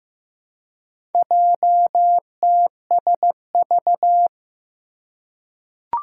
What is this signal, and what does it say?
Morse code, heard as a keyed tone, about 15 wpm, 700 Hz. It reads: JTSV
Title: Four Challenging Letter-Number Combinations 15wpm